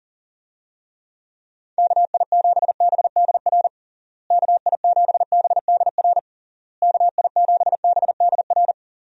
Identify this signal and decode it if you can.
Morse code, heard as a keyed tone, about 40 wpm, 700 Hz. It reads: KI7BDR KI7BDR KI7BDR
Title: USA Call Signs - 6 Characters - Repeated 3x - 40wpm